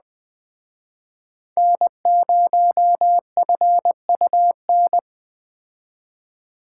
Morse code, heard as a keyed tone, about 20 wpm, 700 Hz. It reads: N0FUN
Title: Speed Racing - USA Call Signs 17wpm